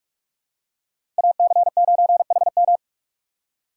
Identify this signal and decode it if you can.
Morse code, heard as a keyed tone, about 46 wpm, 700 Hz. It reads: AC9HM
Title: USA Call Signs - 3x Character Spacing - 45wpm